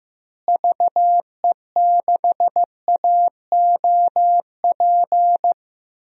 Morse code, heard as a keyed tone, about 15 wpm, 700 Hz. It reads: VE6AOP
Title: DXCC-100 Call Signs - 6 Characters - Repeated 2x - 15wpm